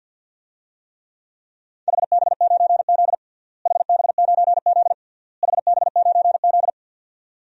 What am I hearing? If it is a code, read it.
Morse code, heard as a keyed tone, about 51 wpm, 700 Hz. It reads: HB9Z HB9Z HB9Z
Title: DXCC-100 Call Signs - 4 Characters - Repeated 3x - 50wpm